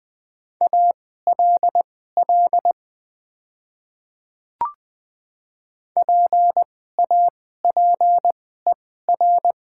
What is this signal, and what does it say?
Morse code, heard as a keyed tone, about 20 wpm, spaced as for 15 wpm, 700 Hz. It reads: ALL PAPER
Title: 2x Character Spacing - Top 500 Words - 20wpm